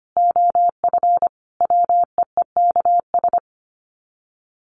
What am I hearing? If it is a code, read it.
Morse code, heard as a keyed tone, about 25 wpm, 700 Hz. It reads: OF WEEKS
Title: Sets of 5 Words Encore2 25wpm